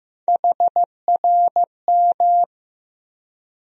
Morse code, heard as a keyed tone, about 15 wpm, 700 Hz. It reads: HRM
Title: USA Call Sign Postfixes - 3 Letters 15wpm